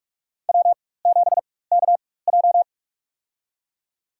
Morse code, heard as a keyed tone, about 46 wpm, spaced as for 21 wpm, 700 Hz. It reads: W7KJ
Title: USA Call Signs - 4 Characters - 4x Character Spacing - 45wpm